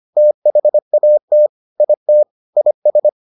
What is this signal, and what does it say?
Morse code, heard as a keyed tone, about 25 wpm, 600 Hz. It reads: THAT IT IS